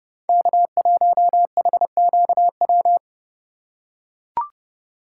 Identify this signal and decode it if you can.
Morse code, heard as a keyed tone, about 30 wpm, 700 Hz. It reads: K1HQW
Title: USA Call Signs - 5 Characters - 2x Character Spacing - 30wpm